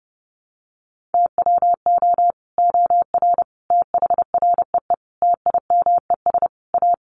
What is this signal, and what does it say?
Morse code, heard as a keyed tone, about 30 wpm, 700 Hz. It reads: TWO OR THREE TIMES A
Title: Sets of 5 Words Encore2 30wpm